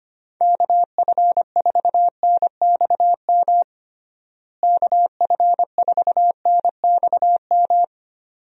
Morse code, heard as a keyed tone, about 25 wpm, 700 Hz. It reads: KF4NXM KF4NXM
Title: USA Call Signs - 6 Characters - Repeated 2x - 25wpm